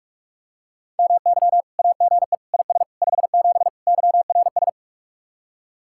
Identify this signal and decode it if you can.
Morse code, heard as a keyed tone, about 46 wpm, 700 Hz. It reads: MY AGE IS 57 YRS